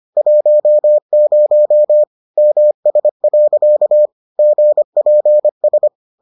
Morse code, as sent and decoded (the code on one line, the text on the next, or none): .---- ----- / -- ... .-.-.- / --. .--. ...
10 MS. GPS